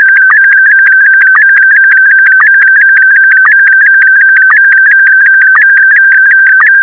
SSTV (slow-scan television), Scottie DX.
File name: SSTV ScottieDX Leo